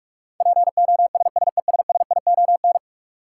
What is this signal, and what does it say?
Morse code, heard as a keyed tone, about 42 wpm, 700 Hz. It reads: POSSESSION
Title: Sentences from Top 1400 Words - New Word Drill Alt - 45wpm